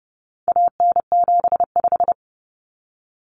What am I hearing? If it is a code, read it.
Morse code, heard as a keyed tone, about 30 wpm, 700 Hz. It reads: AN75
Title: Common Cut Numbers - 4-Digit 30wpm